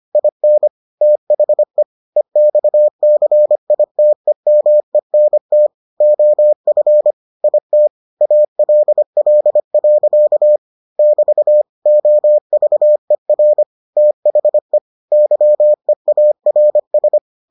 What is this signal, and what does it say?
Morse code, heard as a keyed tone, about 25 wpm, 600 Hz. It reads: IN THE EXCITEMENT OF IT ALL. <BT> OVER THE YEARS